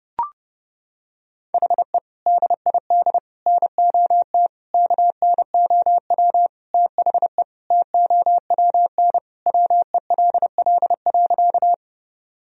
Morse code, heard as a keyed tone, about 32 wpm, 700 Hz. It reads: HE DID NOT KNOW THE TOWN WELL.
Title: Sentences from Top 1000 Words - Review Part 2 - 30wpm